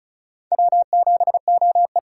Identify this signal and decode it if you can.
Morse code, heard as a keyed tone, about 46 wpm, 700 Hz. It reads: W7OE